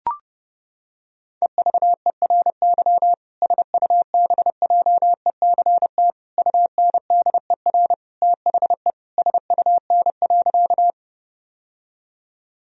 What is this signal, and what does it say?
Morse code, heard as a keyed tone, about 30 wpm, 700 Hz. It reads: EVERY SUBJECT UNDER THE SUN.
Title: Sentences from Top 1000 Words - Review Part 1 - 30wpm